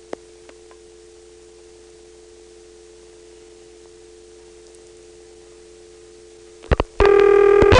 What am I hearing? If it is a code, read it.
Dial tone (North America), 350 + 440 Hz.
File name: phonetrips / PrinceWmCoVA5HQ